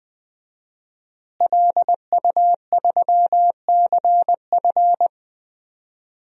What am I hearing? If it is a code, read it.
Morse code, heard as a keyed tone, about 19 wpm, 700 Hz. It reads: LU3CF
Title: DXCC-100 Call Signs - 5 Characters - Repeated 3x - 20wpm